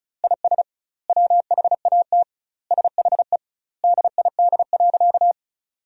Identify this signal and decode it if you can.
Morse code, heard as a keyed tone, about 36 wpm, 700 Hz. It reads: IS WHAT SHE DID.
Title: 2x Word-Spacing - Sentences from Top 100 Words - 35wpm